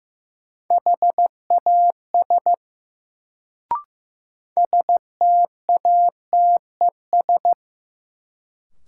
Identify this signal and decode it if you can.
Morse code, heard as a keyed tone, about 15 wpm, 700 Hz. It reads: HAS STATES
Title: Sets of 2 Words - New Word Drill - 15wpm